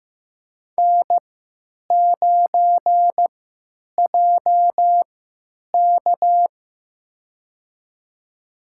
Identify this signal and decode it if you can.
Morse code, heard as a keyed tone, about 15 wpm, spaced as for 9 wpm, 700 Hz. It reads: N9JK
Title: USA Call Signs - 4 Characters - 3x Character Spacing - 15wpm